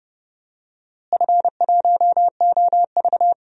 Morse code, heard as a keyed tone, about 30 wpm, 700 Hz. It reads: F1OV
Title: DXCC-100 Call Signs - 4 Characters - Repeated 3x - 30wpm